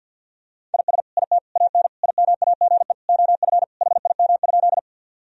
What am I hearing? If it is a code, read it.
Morse code, heard as a keyed tone, about 50 wpm, 700 Hz. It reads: IS IT AN IMAGE OF HIM?
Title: Sentences from Top 1000 Words - Review Part 4 - 50wpm